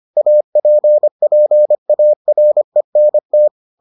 Morse code, heard as a keyed tone, about 25 wpm, 600 Hz. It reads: APPARENT